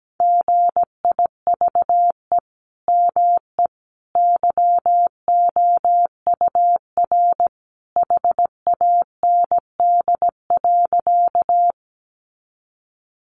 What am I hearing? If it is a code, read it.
Morse code, heard as a keyed tone, about 17 wpm, 700 Hz. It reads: GIVE ME YOUR HAND.
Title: Sentences from Top 200 Words 17wpm